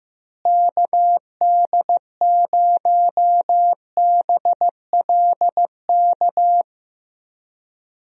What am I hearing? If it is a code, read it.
Morse code, heard as a keyed tone, about 15 wpm, 700 Hz. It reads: KD0BLK
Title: USA Call Signs - 6 Characters - Repeated 3x - 15wpm